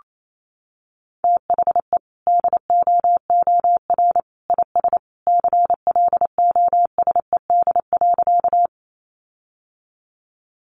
Morse code, heard as a keyed tone, about 28 wpm, 700 Hz. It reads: THE DOOR IS CLOSED.
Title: Sentences from Top 1400 Words 28wpm